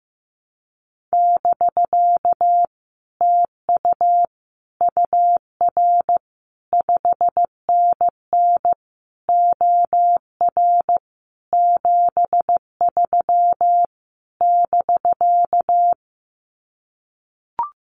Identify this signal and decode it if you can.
Morse code, heard as a keyed tone, about 15 wpm, 700 Hz. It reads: <BK> TU UR 5NN OR 73 <BK>
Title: Common QSO Phrases Encore 15wpm